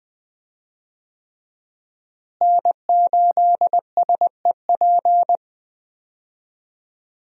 Morse code, heard as a keyed tone, about 20 wpm, 700 Hz. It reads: N8SEP
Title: USA Call Signs - 5 Characters - Repeated 3x - 20wpm